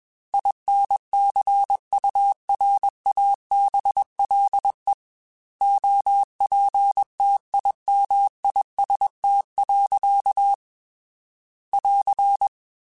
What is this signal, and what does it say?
Morse code, heard as a keyed tone, about 21 wpm, 800 Hz. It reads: INCURABLE OPTIMIST. <AR>
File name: Quotes / Quotes-mp3s / Optimist @20